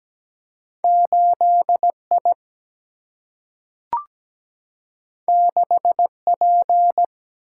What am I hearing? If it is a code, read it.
Morse code, heard as a keyed tone, about 17 wpm, 700 Hz. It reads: 8I 6P
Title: Single-Letter Single-Number 17wpm